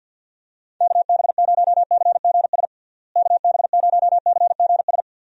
Morse code, heard as a keyed tone, about 46 wpm, 700 Hz. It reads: KB0CGS KB0CGS
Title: USA Call Signs - 6 Characters - Repeated 2x - 50wpm